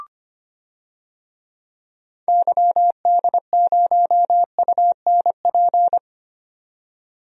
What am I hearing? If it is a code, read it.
Morse code, heard as a keyed tone, about 25 wpm, 700 Hz. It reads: YD0UNP
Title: DXCC-100 Call Signs - 6 Characters - 25wpm